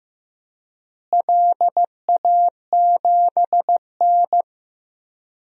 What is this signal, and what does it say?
Morse code, heard as a keyed tone, about 15 wpm, 700 Hz. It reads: LA7N
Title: DXCC-100 Call Signs - 4 Characters - Repeated 3x - 15wpm